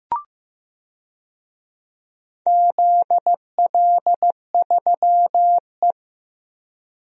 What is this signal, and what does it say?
Morse code, heard as a keyed tone, about 16 wpm, 700 Hz. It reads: ZL3E
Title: DXCC-100 Call Signs - 4 Characters - Repeated 3x - 15wpm